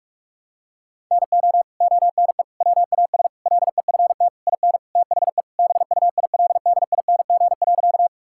Morse code, heard as a keyed tone, about 41 wpm, 700 Hz. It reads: NO ONE WAS LEFT IN THE BUILDING.
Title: Sentences from Top 1000 Words - Review Part 5 - 45wpm